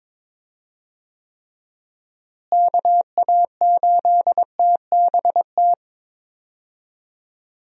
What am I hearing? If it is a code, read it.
Morse code, heard as a keyed tone, about 22 wpm, 700 Hz. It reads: KA8TBT
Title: USA Call Signs - 6 Characters - Repeated 2x - 22wpm